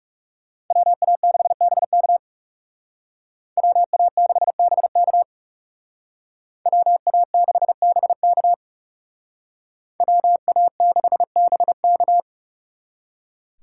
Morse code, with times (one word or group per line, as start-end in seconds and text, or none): 0.7-2.2 WA6BK
3.6-5.2 WA6BK
6.7-8.5 WA6BK
10.0-12.2 WA6BK